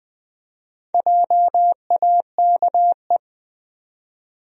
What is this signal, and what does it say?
Morse code, heard as a keyed tone, about 20 wpm, 700 Hz. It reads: JAKE
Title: USA Common Names - Max 4 Letters - 20wpm